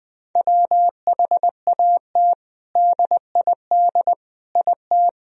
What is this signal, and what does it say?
Morse code, heard as a keyed tone, about 21 wpm, 700 Hz. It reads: WHAT DID IT
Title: Sentences from Top 1000 Words - Review Part 2 - 20wpm